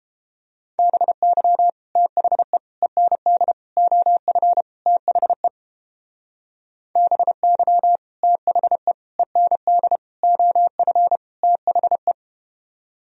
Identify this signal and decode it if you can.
Morse code, heard as a keyed tone, about 31 wpm, 700 Hz. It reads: BY THE END OF THE BY THE END OF THE